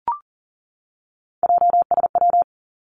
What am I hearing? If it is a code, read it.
Morse code, heard as a keyed tone, about 42 wpm, 690 Hz. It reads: JSW
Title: Three Challenging Letter-Number Combinations 40wpm